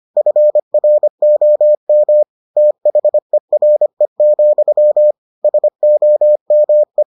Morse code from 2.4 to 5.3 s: THERE,

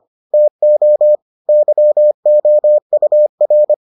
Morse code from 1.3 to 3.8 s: YOUR

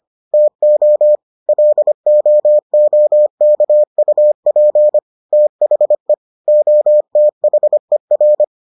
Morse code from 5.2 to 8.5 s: THE OTHER

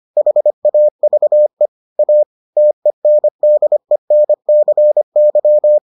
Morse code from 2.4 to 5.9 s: TENDENCY

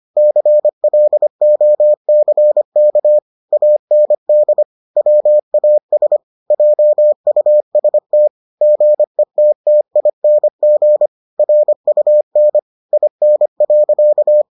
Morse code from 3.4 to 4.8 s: AND

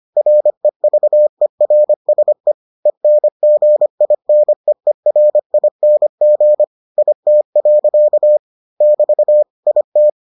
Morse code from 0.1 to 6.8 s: REVERSE ENGINEERING

